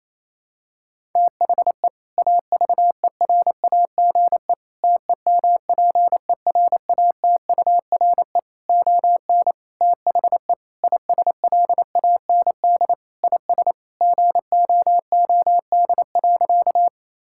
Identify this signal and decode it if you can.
Morse code, heard as a keyed tone, about 27 wpm, 700 Hz. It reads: THE AVERAGE TEMPERATURE ON THE ISLAND IS GOOD.
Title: Sentences from Top 1000 Words - Review Part 6 - 28wpm